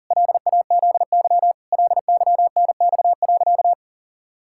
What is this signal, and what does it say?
Morse code, heard as a keyed tone, about 37 wpm, 700 Hz. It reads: LAZY LYNX.
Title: Pangram Challenge v2 - 40wpm